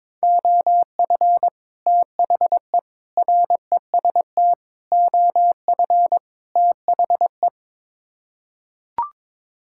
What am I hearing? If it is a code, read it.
Morse code, heard as a keyed tone, about 22 wpm, 700 Hz. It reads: OF THE REST OF THE
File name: practice / 22wpm / Build A Phrase - 5-Word Phrases - 22wpm